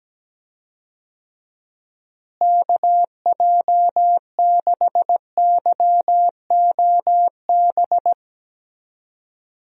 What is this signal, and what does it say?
Morse code, heard as a keyed tone, about 17 wpm, 700 Hz. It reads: KJ6YOB
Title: USA Call Signs - 6 Characters - Repeated 2x - 17wpm